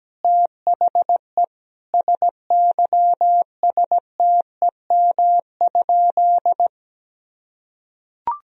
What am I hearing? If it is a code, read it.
Morse code, heard as a keyed tone, about 17 wpm, 700 Hz. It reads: THE SYSTEM?
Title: Sentences from Top 1000 Words - Review Part 3 - 17wpm